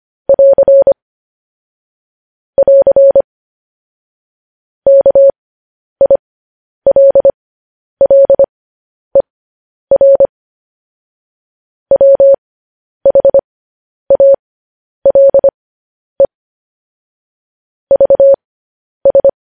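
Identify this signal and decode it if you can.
Morse code, heard as a keyed tone, about 25 wpm, spaced as for 10 wpm, 550 Hz. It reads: <AR> <AR> KILLER WHALE VS
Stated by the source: Morse code transcription of news headlines sent at 10 wpm.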